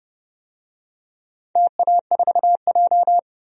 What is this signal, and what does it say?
Morse code, heard as a keyed tone, about 30 wpm, 700 Hz. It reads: TA4J
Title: DXCC-100 Call Signs - 4 Characters - 30wpm